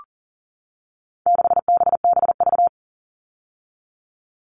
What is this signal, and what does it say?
Morse code, heard as a keyed tone, about 40 wpm, 700 Hz. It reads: 6BBV
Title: All Cut Numbers - 4-Digit 40wpm